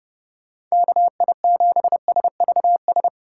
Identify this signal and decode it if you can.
Morse code, heard as a keyed tone, about 32 wpm, 700 Hz. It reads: KI7SVS